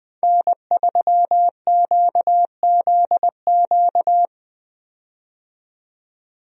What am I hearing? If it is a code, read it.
Morse code, heard as a keyed tone, about 20 wpm, 700 Hz. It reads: N3QZQ
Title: USA Call Signs - 5 Characters - 20wpm